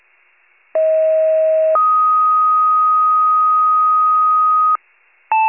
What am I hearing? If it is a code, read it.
Two-tone page, heard as a one-second tone, then a three-second tone, about 640 then 1250 Hz. It is heard from 0.7 to 4.8 s.
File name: pager / 38 FIRE / wcalert